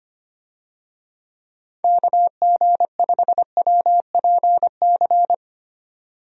Morse code, heard as a keyed tone, about 25 wpm, 700 Hz. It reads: KG5WPC
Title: USA Call Signs - 6 Characters - 25wpm